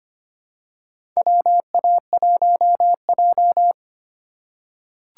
Morse code, heard as a keyed tone, about 25 wpm, 700 Hz. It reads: WA1J
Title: USA Call Signs - 4x Character Spacing - 25wpm